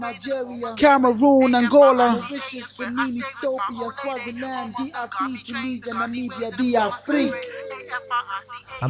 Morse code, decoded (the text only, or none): none